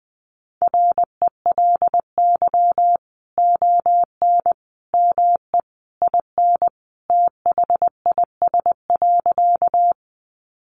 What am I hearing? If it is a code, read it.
Morse code, heard as a keyed tone, about 20 wpm, 700 Hz. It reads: RELY ON ME IN THIS.
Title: Sentences from Top 1200 Words 20wpm